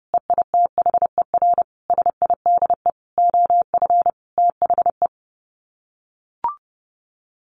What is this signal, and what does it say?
Morse code, heard as a keyed tone, about 28 wpm, 700 Hz. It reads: EITHER SIDE OF THE
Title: Sets of 5 Words 30wpm